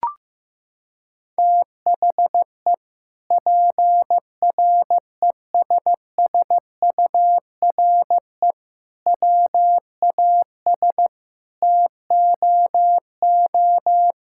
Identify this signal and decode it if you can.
Morse code, heard as a keyed tone, about 15 wpm, 700 Hz. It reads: THE PRESSURE WAS TOO
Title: Sentences from Top 1000 Words - Review Part 6 - 15wpm